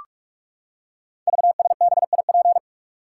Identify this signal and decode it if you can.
Morse code, heard as a keyed tone, about 46 wpm, 700 Hz. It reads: VSBIP
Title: Code Groups - 5 Letter:Numbers - 45wpm